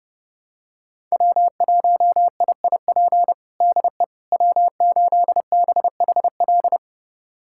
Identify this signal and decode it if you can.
Morse code, heard as a keyed tone, about 32 wpm, 700 Hz. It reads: W1IIP DE W8BHL
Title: USA Callsign DE Callsign - 5 Characters - 30wpm